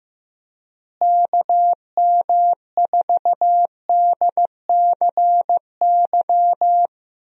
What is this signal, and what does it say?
Morse code, heard as a keyed tone, about 15 wpm, 700 Hz. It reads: KM4DCY
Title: USA Call Signs - 6 Characters - Repeated 2x - 15wpm